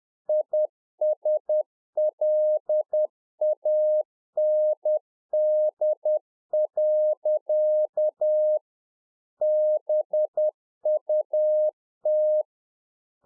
Morse code, heard as a keyed tone, about 10 wpm, 610 Hz. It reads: ISLAND. BUT